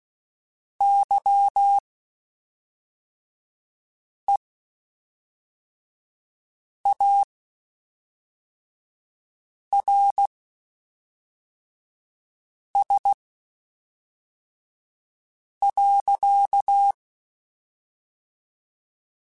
Morse code, heard as a keyed tone, about 16 wpm, spaced as for 3 wpm, 800 Hz. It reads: YEARS.
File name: (spaced) / QSOs Spaced-mp3s / kc7haa @5 sp